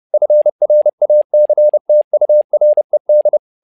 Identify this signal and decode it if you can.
Morse code, heard as a keyed tone, about 30 wpm, 600 Hz. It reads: FRACTURED
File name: learning center / tower trouble / tower trouble 30wpm